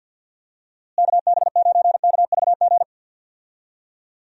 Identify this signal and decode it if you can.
Morse code, heard as a keyed tone, about 49 wpm, 700 Hz. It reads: KB9KVG
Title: USA Call Signs - 6 Characters - Repeated 3x - 50wpm